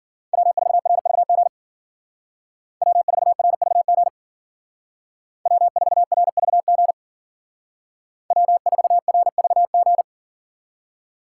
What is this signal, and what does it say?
Morse code, heard as a keyed tone, about 50 wpm, 700 Hz. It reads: W4RVG W4RVG W4RVG W4RVG
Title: Speed Racing - USA 1x3 Call Signs 40wpm